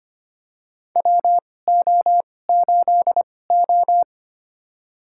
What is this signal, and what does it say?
Morse code, heard as a keyed tone, about 25 wpm, spaced as for 18 wpm, 700 Hz. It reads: WO8O
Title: USA Call Signs - 4 Characters - 2x Character Spacing - 25wpm